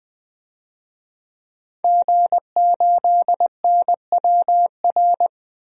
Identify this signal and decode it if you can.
Morse code, heard as a keyed tone, about 21 wpm, 700 Hz. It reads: G8NWR
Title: DXCC-100 Call Signs - 5 Characters - 20wpm